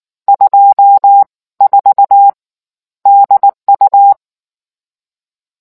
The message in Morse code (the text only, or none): 24 DU